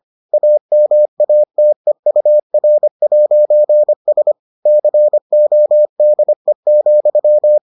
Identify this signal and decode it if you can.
Morse code, heard as a keyed tone, about 25 wpm, 600 Hz. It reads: AMATEUR'S CODE,